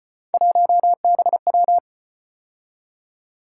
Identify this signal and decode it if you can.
Morse code, heard as a keyed tone, about 36 wpm, 700 Hz. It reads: 1BW